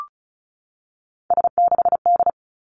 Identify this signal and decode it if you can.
Morse code, heard as a keyed tone, about 36 wpm, 700 Hz. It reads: S6D
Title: Three Challenging Letter-Number Combinations 35wpm